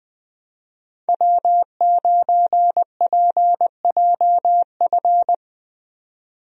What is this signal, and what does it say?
Morse code, heard as a keyed tone, about 20 wpm, 700 Hz. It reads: W9PJF
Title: USA Call Signs - 5 Characters - 20wpm